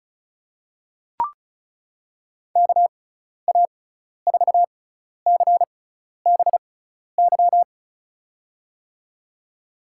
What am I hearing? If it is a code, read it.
Morse code, heard as a keyed tone, about 36 wpm, spaced as for 12 wpm, 690 Hz. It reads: KA4CBY
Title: USA Call Signs - 6 Characters - 6x Character Spacing - 35wpm